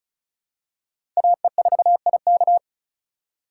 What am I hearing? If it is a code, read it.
Morse code, heard as a keyed tone, about 36 wpm, 700 Hz. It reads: AE4IK
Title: USA Call Signs - 5 Characters - 3x Character Spacing - 35wpm